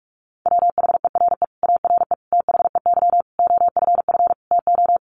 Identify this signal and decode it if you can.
Morse code, heard as a keyed tone, about 44 wpm, 700 Hz. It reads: WHERE ARE THEY OFF TO
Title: Sentences from Top 200 Words 45wpm